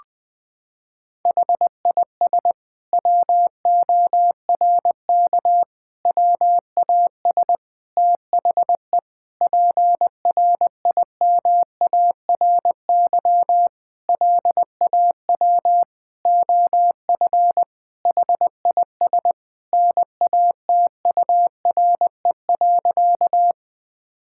Morse code, heard as a keyed tone, about 20 wpm, 700 Hz. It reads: HIS WORK WAS THE PRIMARY LAW OF HIS NATURE.
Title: Sentences from Top 1000 Words - Review Part 5 - 20wpm